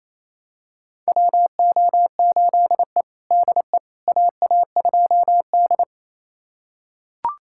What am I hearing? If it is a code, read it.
Morse code, heard as a keyed tone, about 28 wpm, 700 Hz. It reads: WO8E DE AA2D
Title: USA Callsign DE Callsign - 4 Characters - 28wpm